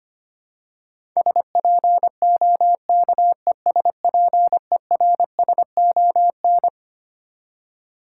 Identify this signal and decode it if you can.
Morse code, heard as a keyed tone, about 25 wpm, 700 Hz. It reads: SPOKESPERSON
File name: Twelve Letter Words - 25wpm